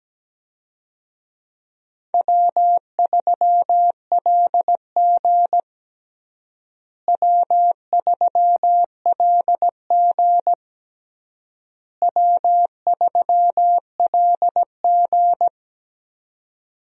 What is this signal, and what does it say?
Morse code, heard as a keyed tone, about 17 wpm, 700 Hz. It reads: W3LG W3LG W3LG
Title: USA Call Signs - 4 Characters - Repeated 3x - 17wpm